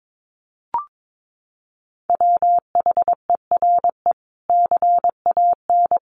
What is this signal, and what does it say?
Morse code, heard as a keyed tone, about 22 wpm, 700 Hz. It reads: WHERE CAN
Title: Sentences from Top 1500 Words 22wpm